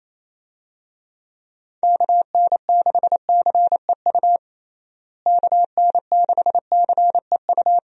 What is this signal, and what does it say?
Morse code, heard as a keyed tone, about 28 wpm, 700 Hz. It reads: KN6CEU KN6CEU
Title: USA Call Signs - 6 Characters - Repeated 2x - 28wpm